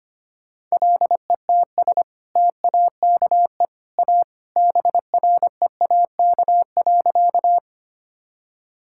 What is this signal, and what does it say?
Morse code, heard as a keyed tone, about 25 wpm, 700 Hz. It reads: LETS TAKE A BREAK.
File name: practice / 25wpm / Sentences with 2-Word Phrases - 25wpm